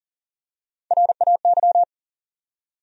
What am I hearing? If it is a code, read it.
Morse code, heard as a keyed tone, about 40 wpm, 700 Hz. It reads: RAY
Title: USA Common Names - Max 3 Letters - 40wpm